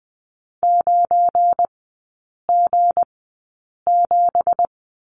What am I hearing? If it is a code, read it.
Morse code, heard as a keyed tone, about 20 wpm, spaced as for 8 wpm, 690 Hz. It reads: 9G7
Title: 172 Lesson 28e - Number 9 - Three Characters 20wpm